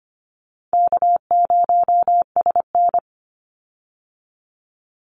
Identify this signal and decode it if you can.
Morse code, heard as a keyed tone, about 25 wpm, 700 Hz. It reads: K0SN
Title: Call Signs - 25wpm